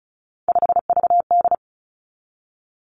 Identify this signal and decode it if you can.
Morse code, heard as a keyed tone, about 36 wpm, 700 Hz. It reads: 5VD
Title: Three Challenging Letter-Number Combinations 35wpm